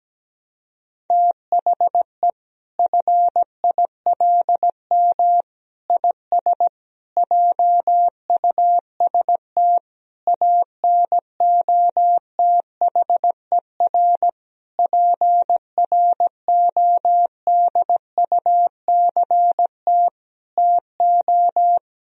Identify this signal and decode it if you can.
Morse code, heard as a keyed tone, about 17 wpm, 700 Hz. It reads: THE FILM IS JUST ANOTHER PRODUCT TO
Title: Sentences from Top 1000 Words - Review Part 3 - 17wpm